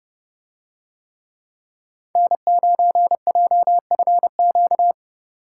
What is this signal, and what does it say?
Morse code, heard as a keyed tone, about 27 wpm, 700 Hz. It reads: N9JFQ